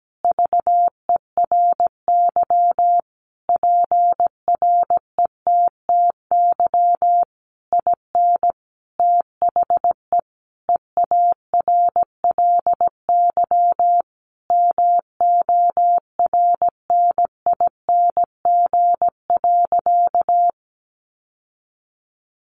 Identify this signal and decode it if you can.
Morse code, heard as a keyed tone, about 17 wpm, 700 Hz. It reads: VERY PRETTY IN THE EARLY MORNING.
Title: Sentences from Top 1300 Words 17wpm